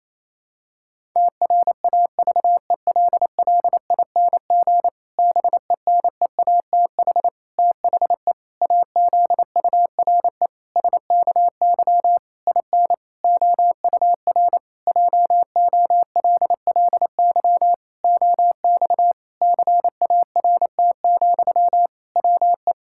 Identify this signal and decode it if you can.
Morse code, heard as a keyed tone, about 28 wpm, 700 Hz. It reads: TRAVELLING BENEATH THE AZURE SKY IN OUR JOLLY OX CART, WE
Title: Pangram Challenge v2 - 28wpm